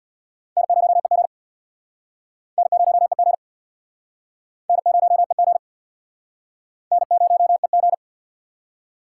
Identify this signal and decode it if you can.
Morse code, heard as a keyed tone, about 67 wpm, 700 Hz. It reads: N0EG N0EG N0EG N0EG